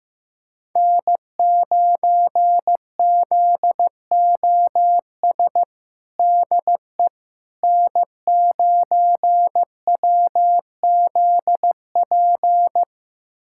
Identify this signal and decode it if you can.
Morse code, heard as a keyed tone, about 15 wpm, 700 Hz. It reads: N9ZOS DE N9WZP
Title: USA Callsign DE Callsign - 5 Characters - 15wpm